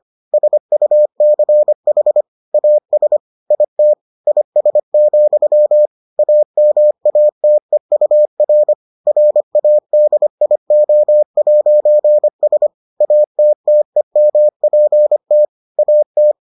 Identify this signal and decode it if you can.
Morse code, heard as a keyed tone, about 25 wpm, 600 Hz. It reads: SUCH AS IT IS, AMATEUR RADIO'S ATTEMPT AT